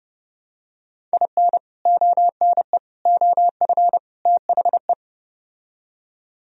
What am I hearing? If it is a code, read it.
Morse code, heard as a keyed tone, about 30 wpm, 700 Hz. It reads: IN ONE OF THE